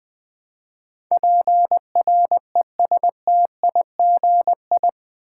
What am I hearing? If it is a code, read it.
Morse code, heard as a keyed tone, about 20 wpm, 700 Hz. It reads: PRESTIGI
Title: Build A Word - 4 Letters At A Time - 11-Letter Words - 20wpm